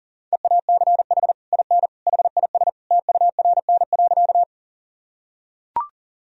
Morse code, heard as a keyed tone, about 40 wpm, 700 Hz. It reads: EACH IN HIS TURN.
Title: Sentences from Top 1000 Words - Review Part 1 - 40wpm